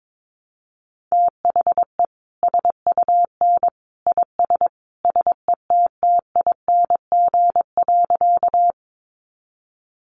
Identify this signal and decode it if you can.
Morse code, heard as a keyed tone, about 22 wpm, 700 Hz. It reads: THE SUN IS SETTING.
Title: Sentences from Top 1300 Words 22wpm